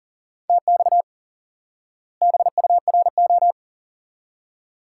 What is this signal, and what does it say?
Morse code, heard as a keyed tone, about 40 wpm, 700 Hz. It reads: TX BURO
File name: IQE Course 3 - Lesson 89 - YAGI - Two Elements - 40wpm